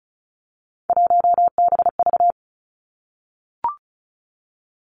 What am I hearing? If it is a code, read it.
Morse code, heard as a keyed tone, about 36 wpm, 700 Hz. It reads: 1BV
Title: Three Challenging Letter-Number Combinations 35wpm